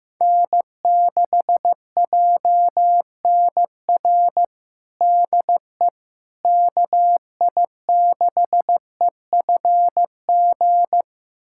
Morse code, heard as a keyed tone, about 15 wpm, 700 Hz. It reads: N6JNR DE KI6EFG
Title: USA Callsign DE Callsign 15wpm